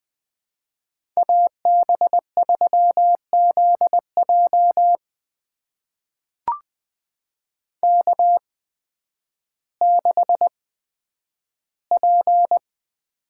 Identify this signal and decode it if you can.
Morse code, heard as a keyed tone, about 20 wpm, spaced as for 11 wpm, 700 Hz. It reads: AB3ZJ K6P
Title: USA Call Signs - 8x Character Spacing - 20wpm